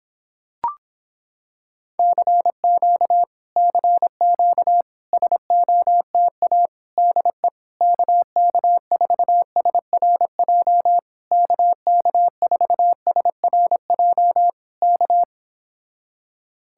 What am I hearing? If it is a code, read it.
Morse code, heard as a keyed tone, about 26 wpm, 700 Hz. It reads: CQ CQ SOTA DE KK4SRJ KK4SRJ K